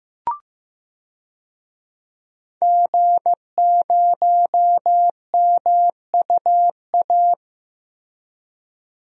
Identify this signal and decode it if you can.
Morse code, heard as a keyed tone, about 15 wpm, 700 Hz. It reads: G0MUA
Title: DXCC-100 Call Signs - 5 Characters - Repeated 3x - 15wpm